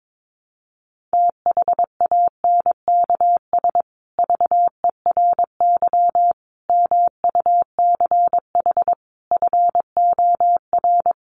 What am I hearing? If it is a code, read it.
Morse code, heard as a keyed tone, about 22 wpm, 700 Hz. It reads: THANKS VERY MUCH FOR
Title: Sets of 5 Words Encore2 22wpm